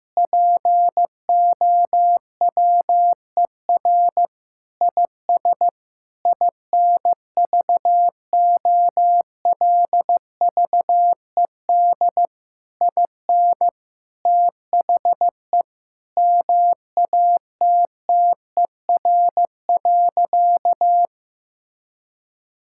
Morse code, heard as a keyed tone, about 14 wpm, 700 Hz. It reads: POWER IS INVOLVED IN THE MATTER.
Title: Sentences from Top 1000 Words - Review Part 5 - 15wpm